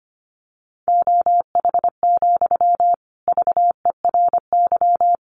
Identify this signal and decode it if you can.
Morse code, heard as a keyed tone, about 25 wpm, 700 Hz. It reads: OH, VERY
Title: Sentences from Top 900 Words 25wpm